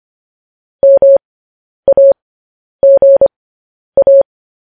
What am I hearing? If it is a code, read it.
Morse code, heard as a keyed tone, about 25 wpm, spaced as for 10 wpm, 550 Hz. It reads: MAGA
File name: podcast / News headlines-1x-25x10